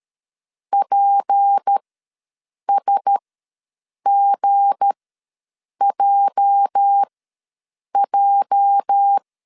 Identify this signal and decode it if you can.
Morse code, heard as a keyed tone, about 13 wpm, spaced as for 7 wpm, 800 Hz. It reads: PSGJJ
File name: LEZIONI / 8 YJ45 / 8 YJ ws 12wpm 3342